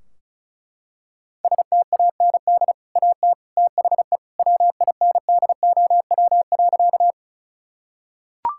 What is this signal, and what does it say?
Morse code, heard as a keyed tone, about 36 wpm, 700 Hz. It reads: STAND AT THE WINDOW.
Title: Sentences from Top 1600 Words 35wpm